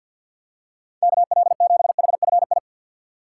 Morse code, heard as a keyed tone, about 50 wpm, 700 Hz. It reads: KL7HFI
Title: K1USN SST QSO Practice - 50wpm